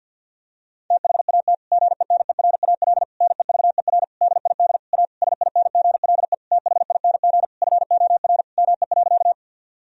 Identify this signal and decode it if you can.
Morse code, heard as a keyed tone, about 50 wpm, 700 Hz. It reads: THAT GENERAL NEVER DID A SINGLE THING FOR ME.
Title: Sentences from Top 1000 Words - Review Part 4 - 50wpm